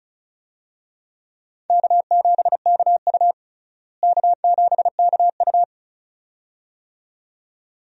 Morse code, heard as a keyed tone, about 36 wpm, 700 Hz. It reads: K7KU K7KU
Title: USA Call Signs - 4 Characters - Repeated 2x - 35wpm